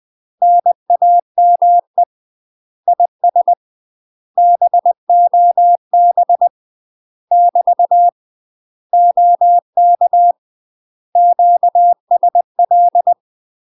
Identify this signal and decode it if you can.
Morse code, heard as a keyed tone, about 20 wpm, 700 Hz. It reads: NAME IS BOB <BT> OK QSL
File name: UK MASTER QSOs TRAINING / 20WPM / UK MASTER QSOs 0010